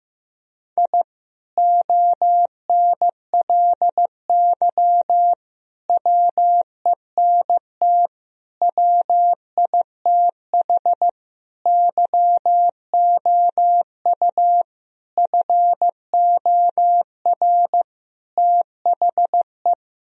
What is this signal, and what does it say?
Morse code, heard as a keyed tone, about 15 wpm, 700 Hz. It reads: I ONLY WENT WITH YOU FOR THE
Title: Sentences from Top 1000 Words - Review Part 1 - 15wpm